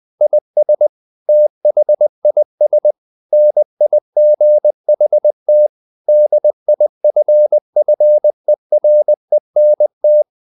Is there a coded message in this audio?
Yes: Morse code from 0.2 to 10.2 s, heard as a keyed tone, about 20 wpm, 600 Hz. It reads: IS THIS NIGHT DIFFERENT